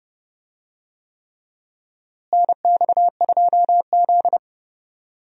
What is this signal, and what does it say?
Morse code, heard as a keyed tone, about 30 wpm, 700 Hz. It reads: NX2Z
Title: USA Call Signs - 4 Characters - Repeated 3x - 30wpm